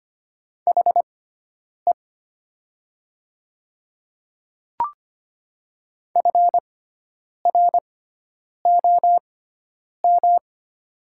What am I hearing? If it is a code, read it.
Morse code, heard as a keyed tone, about 25 wpm, spaced as for 9 wpm, 700 Hz. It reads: HE FROM
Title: 6x Character Spacing - Top 100 Words - 25wpm